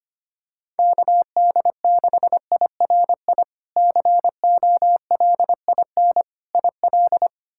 Morse code, heard as a keyed tone, about 25 wpm, 700 Hz. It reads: KD6IRI COLIN IL
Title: K1USN SST QSO Practice - 25wpm